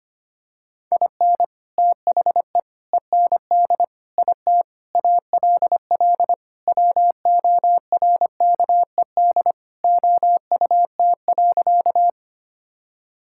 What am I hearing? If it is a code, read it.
Morse code, heard as a keyed tone, about 25 wpm, 700 Hz. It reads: IN THE END IT ALL WORKED OUT.
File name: practice / 25wpm / Sentences with 3-Word Phrases - 25wpm